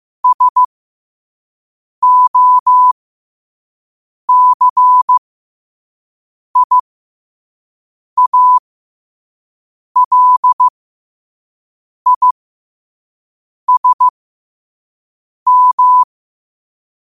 Morse code, heard as a keyed tone, about 15 wpm, spaced as for 5 wpm, 1000 Hz. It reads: SOCIALISM
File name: qotdfiles / 5wpm